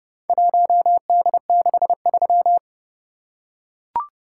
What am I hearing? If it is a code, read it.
Morse code, heard as a keyed tone, about 28 wpm, 700 Hz. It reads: 1D63